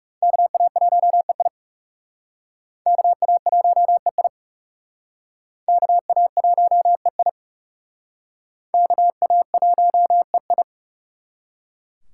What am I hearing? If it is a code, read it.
Morse code, heard as a keyed tone, about 35 wpm, 700 Hz. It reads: KA1EI KA1EI KA1EI KA1EI
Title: Speed Racing - USA 2x2 Call Signs 30wpm